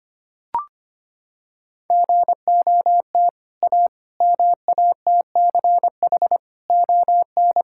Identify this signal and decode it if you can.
Morse code, heard as a keyed tone, about 25 wpm, 700 Hz. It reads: GOT A MATCH ON
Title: Sentences from Top 1600 Words 25wpm